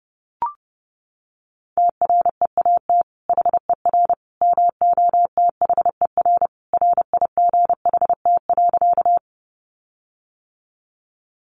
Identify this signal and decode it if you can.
Morse code, heard as a keyed tone, about 30 wpm, 700 Hz. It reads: TREAT HER MOTHER RIGHT.
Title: Sentences from Top 1300 Words 30wpm